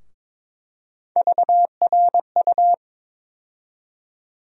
Morse code, heard as a keyed tone, about 22 wpm, 700 Hz. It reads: VRU
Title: USA Call Sign Postfixes - 3 Letters 22wpm